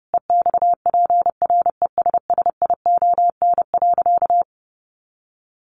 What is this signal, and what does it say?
Morse code, heard as a keyed tone, about 30 wpm, 700 Hz. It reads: EXPRESSION.
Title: Sentences from Top 1200 Words 30wpm